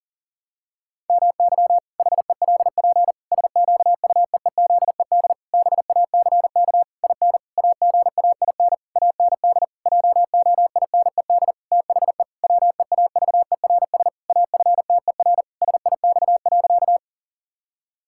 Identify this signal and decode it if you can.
Morse code, heard as a keyed tone, about 40 wpm, 700 Hz. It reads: MY HELP SQUEEZED BACK IN AGAIN AND JOINED THE WEAVERS AFTER SIX.
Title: Pangram Challenge v2 - 40wpm